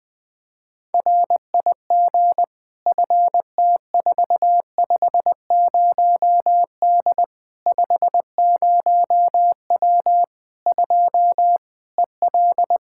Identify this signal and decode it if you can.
Morse code, heard as a keyed tone, about 20 wpm, 700 Hz. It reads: RIG FT450D 50W 2 EL
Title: Practice QSO - Exchange 2 - Info - 20wpm